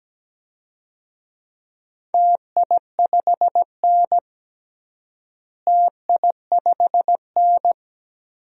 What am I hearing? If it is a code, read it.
Morse code, heard as a keyed tone, about 17 wpm, 700 Hz. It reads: TI5N TI5N
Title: DXCC-100 Call Signs - 4 Characters - Repeated 2x - 17wpm